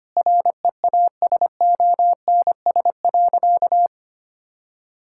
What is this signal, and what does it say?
Morse code, heard as a keyed tone, about 25 wpm, 700 Hz. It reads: REASONS.
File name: practice / 25wpm / Sentences with 4-Word Phrases - 25wpm